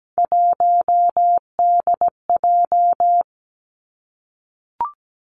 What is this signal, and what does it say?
Morse code, heard as a keyed tone, about 17 wpm, 700 Hz. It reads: 1DJ
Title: Three Challenging Letter-Number Combinations 17wpm